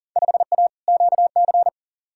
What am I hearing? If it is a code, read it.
Morse code, heard as a keyed tone, about 39 wpm, 700 Hz. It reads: 5A QC
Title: Field Day Exchanges Encore - 40wpm